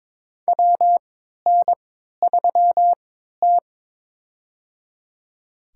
Morse code, heard as a keyed tone, about 23 wpm, spaced as for 13 wpm, 700 Hz. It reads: WN3T
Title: USA Call Signs - 4 Characters - 3x Character Spacing - 22wpm